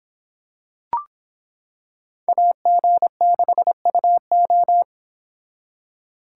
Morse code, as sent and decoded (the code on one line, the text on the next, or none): .- --. -.... ..- ---
AG6UO